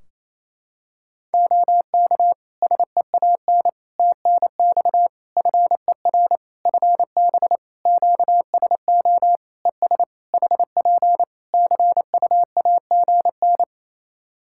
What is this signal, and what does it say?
Morse code, heard as a keyed tone, about 28 wpm, 700 Hz. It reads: OK SEAN TNX FER FB QSO ES HP CUAGN
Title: Practice QSO - Exchange 4 - Ending - 28wpm